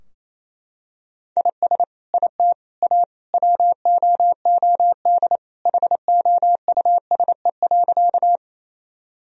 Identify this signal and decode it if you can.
Morse code, heard as a keyed tone, about 28 wpm, 700 Hz. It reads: IS IT A WOOD HOUSE.
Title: Sentences from Top 1000 Words - Review Part 2 - 28wpm